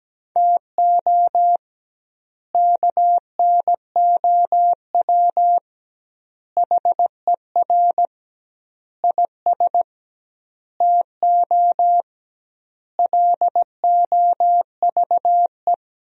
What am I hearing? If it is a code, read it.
Morse code, heard as a keyed tone, about 17 wpm, 700 Hz. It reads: TO KNOW HER IS TO LOVE
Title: 2x Word-Spacing - Sentences from Top 400 Words - 17wpm